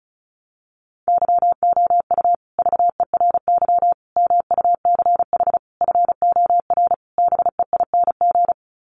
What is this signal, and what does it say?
Morse code, heard as a keyed tone, about 34 wpm, 700 Hz. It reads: YOU VERY MUCH FOR BEING
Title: Sets of 5 Words Encore 35wpm